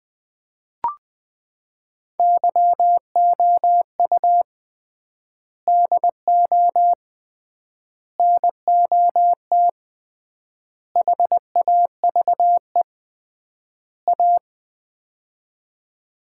Morse code, heard as a keyed tone, about 20 wpm, 700 Hz. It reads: YOU DO NOT HAVE A
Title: 3x Word-Spacing - Sets of 5 Words - 20wpm